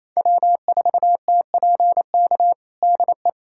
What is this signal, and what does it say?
Morse code, heard as a keyed tone, about 28 wpm, 700 Hz. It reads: W4TPK DE
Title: USA Callsign DE Callsign - 5 Characters - 28wpm